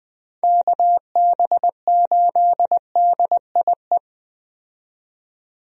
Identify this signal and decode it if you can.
Morse code, heard as a keyed tone, about 21 wpm, 700 Hz. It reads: KB8DIE
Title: USA Call Signs - 6 Characters - 20wpm